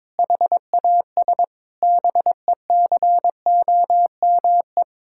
Morse code, heard as a keyed tone, about 22 wpm, 700 Hz. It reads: HAS BECOME
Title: Sentences from Top 1000 Words - Review Part 5 - 22wpm